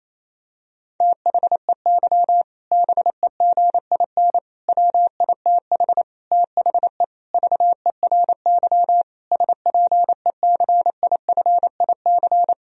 Morse code, heard as a keyed tone, about 28 wpm, 700 Hz. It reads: THEY BEGIN WITH THE VERY SPECIFIC
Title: Sentences from Top 1000 Words - Review Part 3 - 28wpm